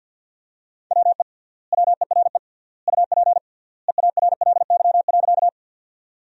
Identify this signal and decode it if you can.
Morse code, heard as a keyed tone, about 50 wpm, 700 Hz. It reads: WE WERE UP EARLY.
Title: 3x Word-Spacing - Sentences from Top 400 Words - 50wpm